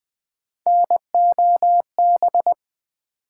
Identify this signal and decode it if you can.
Morse code, heard as a keyed tone, about 20 wpm, 700 Hz. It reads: NOB
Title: Build A Word - 1 Letter At A Time - 3 to 5-Letter Words - 20wpm